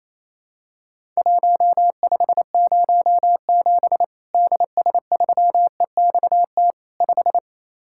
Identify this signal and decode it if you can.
Morse code, heard as a keyed tone, about 28 wpm, 700 Hz. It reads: 1507 DS3EXT 5
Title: Worked All Europe DX Contest - QTC Exchange - 28wpm